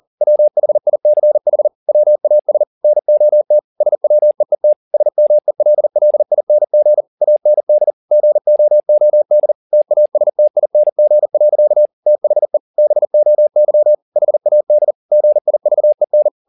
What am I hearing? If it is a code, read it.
Morse code, heard as a keyed tone, about 40 wpm, 600 Hz. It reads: WHICH WAS NOT SWEET SMELLING AND GOOD TASTING. THE BOY HAD GIVEN